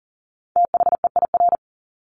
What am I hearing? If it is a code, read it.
Morse code, heard as a keyed tone, about 40 wpm, 700 Hz. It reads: THEIR
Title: Top 200 Words Repeated 40wpm